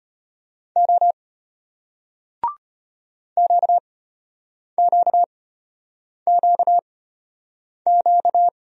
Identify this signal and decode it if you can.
Morse code, heard as a keyed tone, about 34 wpm, spaced as for 8 wpm, 700 Hz. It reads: O QQQQ